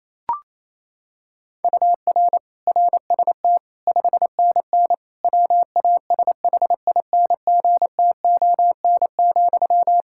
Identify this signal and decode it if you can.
Morse code, heard as a keyed tone, about 28 wpm, 700 Hz. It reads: UR RST 5NN WASHINGTON,
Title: Practice QSO - Exchange 1 - Beginning - 28wpm